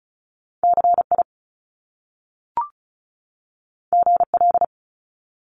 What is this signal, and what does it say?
Morse code, heard as a keyed tone, about 36 wpm, 700 Hz. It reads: CI GL
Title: Top Two Letter Combinations in Words 35wpm